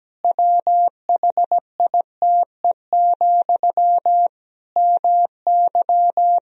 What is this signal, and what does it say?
Morse code, heard as a keyed tone, about 17 wpm, 700 Hz. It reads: WHITE, MY
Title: Sentences from Top 1000 Words - Review Part 1 - 17wpm